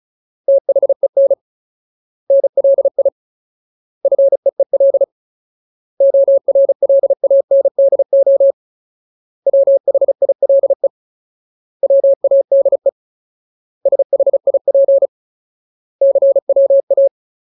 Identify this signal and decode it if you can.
Morse code, heard as a keyed tone, about 34 wpm, 550 Hz. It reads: THEN NLI FEEL ORLANDO WHILE WADE SHIP CWA
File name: Ultimate Word List / 35WPM Ultimate